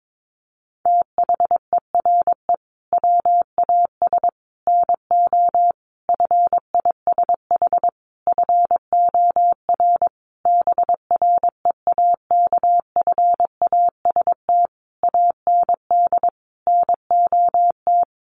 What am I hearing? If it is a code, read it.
Morse code, heard as a keyed tone, about 22 wpm, 700 Hz. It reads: THERE WAS NO FISH FOR BREAKFAST AND NOT
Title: Sentences from Top 1500 Words 22wpm